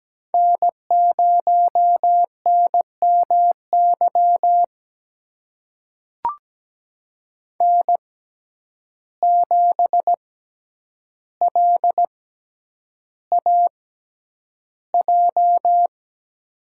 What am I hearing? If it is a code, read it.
Morse code, heard as a keyed tone, about 17 wpm, spaced as for 9 wpm, 700 Hz. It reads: N0NMY N7LAJ
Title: USA Call Signs - 5 Characters - 6x Character Spacing - 17wpm